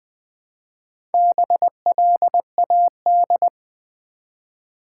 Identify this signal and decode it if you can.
Morse code, heard as a keyed tone, about 20 wpm, 700 Hz. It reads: BLAD
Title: Build A Word - 1 Letter At A Time - 3 to 5-Letter Words - 20wpm